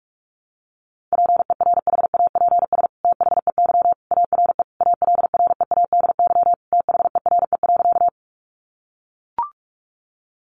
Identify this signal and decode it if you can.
Morse code, heard as a keyed tone, about 44 wpm, 700 Hz. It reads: PERHAPS THEY ARE ALREADY THERE.
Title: Sentences from Top 700 Words 45wpm